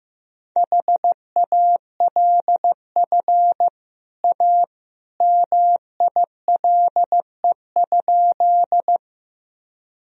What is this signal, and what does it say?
Morse code, heard as a keyed tone, about 15 wpm, 700 Hz. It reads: HALF A MILE?
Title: Sentences from Top 1000 Words - Review Part 1 - 15wpm